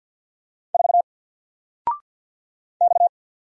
Morse code, heard as a keyed tone, about 50 wpm, 700 Hz. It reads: 4 X
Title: Single Letter-Number 50wpm